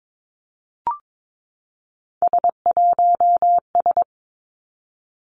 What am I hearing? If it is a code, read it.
Morse code, heard as a keyed tone, about 22 wpm, 700 Hz. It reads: S1S
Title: Three Challenging Letter-Number Combinations 22wpm